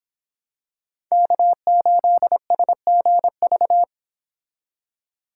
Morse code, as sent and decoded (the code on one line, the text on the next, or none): -.- ---.. ... --. ...-
K8SGV